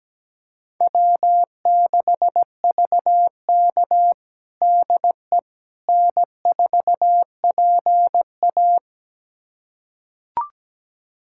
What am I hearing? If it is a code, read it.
Morse code, heard as a keyed tone, about 17 wpm, 700 Hz. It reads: W6VK DE N4PA
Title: USA Callsign DE Callsign - 4 Characters - 17wpm